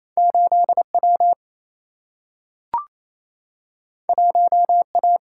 Morse code, heard as a keyed tone, about 28 wpm, 700 Hz. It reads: 8 W 1 A
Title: Single-Letter Single-Number 28wpm